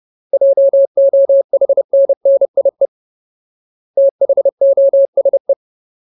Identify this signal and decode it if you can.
Morse code, heard as a keyed tone, about 30 wpm, 550 Hz. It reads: JOHNNIE THOSE
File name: Ultimate Word List / 30WPM Ultimate